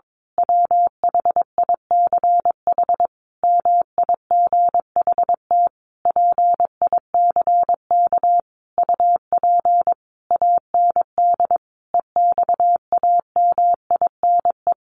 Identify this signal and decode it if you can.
Morse code, heard as a keyed tone, about 22 wpm, 700 Hz. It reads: WHICH MIGHT PICK UP AND EXAMINE
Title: A Sound Of Thunder 22wpm